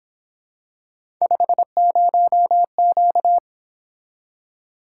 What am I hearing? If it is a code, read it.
Morse code, heard as a keyed tone, about 26 wpm, 700 Hz. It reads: 50Q
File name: Speed-Racing - Three Letter-Number Combinations - 22wpm